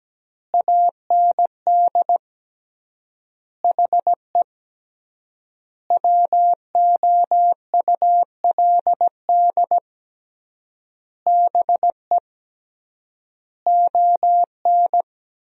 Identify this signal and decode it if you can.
Morse code, heard as a keyed tone, about 17 wpm, 700 Hz. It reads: AND HE WOULD BE ON
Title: 3x Word-Spacing - Sentences from Top 300 Words - 17wpm